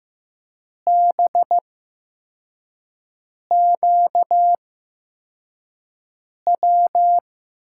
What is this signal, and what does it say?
Morse code, heard as a keyed tone, about 15 wpm, spaced as for 4 wpm, 700 Hz. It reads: BQW
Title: USA Call Signs - 6 Characters - 8x Character Spacing - 15wpm